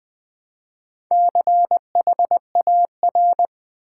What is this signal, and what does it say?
Morse code, heard as a keyed tone, about 20 wpm, 700 Hz. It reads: CHAR
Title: Build A Word - 2 Letters At A Time - 6-Letter Words - 20wpm